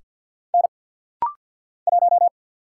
Morse code, heard as a keyed tone, about 49 wpm, 700 Hz. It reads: N 1
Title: Single Letter-Number Alt - Rapid-Fire 50wpm